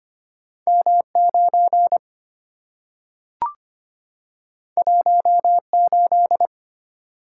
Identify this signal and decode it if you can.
Morse code, heard as a keyed tone, about 25 wpm, 690 Hz. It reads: M9 18
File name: Two Letter-Number Combinations - 25wpm